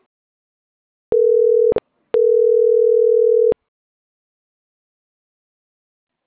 Ringback tone (North America), 440 + 480 Hz.